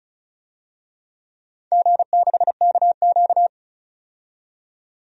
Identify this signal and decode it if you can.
Morse code, heard as a keyed tone, about 36 wpm, 690 Hz. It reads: G6KQ
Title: DXCC-100 Call Signs - 4 Characters - 35wpm